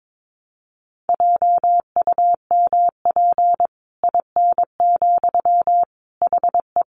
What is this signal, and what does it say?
Morse code, heard as a keyed tone, about 22 wpm, 700 Hz. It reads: JUMP IN, HE
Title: Sentences from Top 1300 Words 22wpm